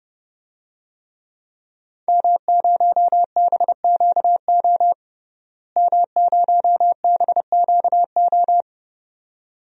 Morse code, heard as a keyed tone, about 31 wpm, 700 Hz. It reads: M0BQO M0BQO
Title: DXCC-100 Call Signs - 5 Characters - Repeated 2x - 30wpm